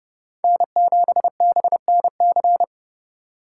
Morse code, heard as a keyed tone, about 29 wpm, 700 Hz. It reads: N7BNC